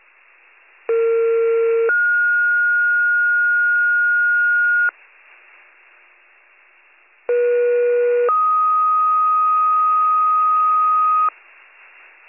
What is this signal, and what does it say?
Two-tone page, heard as a one-second tone, then a three-second tone, about 460 then 1430 Hz, about 480 then 1250 Hz. It is heard from 0.9 to 11.3 s.